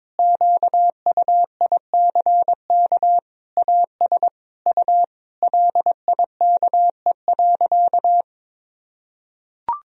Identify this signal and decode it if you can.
Morse code, heard as a keyed tone, about 22 wpm, 700 Hz. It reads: QUICK AS U LIKE.
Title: Sentences from Top 500 Words and Abbreviations 22wpm